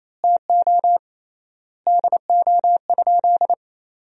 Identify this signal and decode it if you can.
Morse code, heard as a keyed tone, about 28 wpm, 700 Hz. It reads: TO DO?
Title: 3x Word-Spacing - Sentences from Top 400 Words - 28wpm